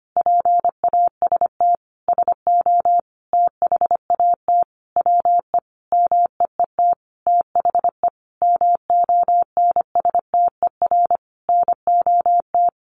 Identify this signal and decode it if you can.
Morse code, heard as a keyed tone, about 25 wpm, 700 Hz. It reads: PAST SO THAT WE MEET THE MONSTER NOT
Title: A Sound Of Thunder 25wpm